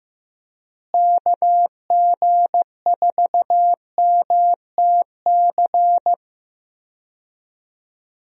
Morse code, heard as a keyed tone, about 15 wpm, 700 Hz. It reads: KG4MTC
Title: USA Call Signs - 6 Characters - Repeated 3x - 15wpm